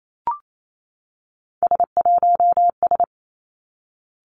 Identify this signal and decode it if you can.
Morse code, heard as a keyed tone, about 28 wpm, 700 Hz. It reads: S1S
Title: Three Challenging Letter-Number Combinations 28wpm